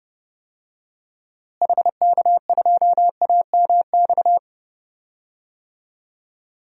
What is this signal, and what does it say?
Morse code, heard as a keyed tone, about 29 wpm, 700 Hz. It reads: HK2AMX
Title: DXCC-100 Call Signs - 6 Characters - Repeated 2x - 30wpm